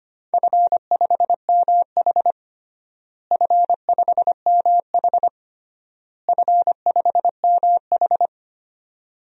Morse code, heard as a keyed tone, about 25 wpm, 700 Hz. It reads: F5MH F5MH F5MH
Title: DXCC-100 Call Signs - 4 Characters - Repeated 3x - 25wpm